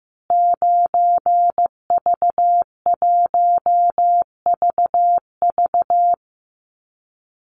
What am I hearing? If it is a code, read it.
Morse code, heard as a keyed tone, about 15 wpm, 700 Hz. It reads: 9V1VV
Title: Call Signs Repeated 2x - 15wpm